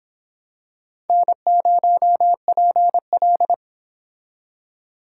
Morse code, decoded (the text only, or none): N0PL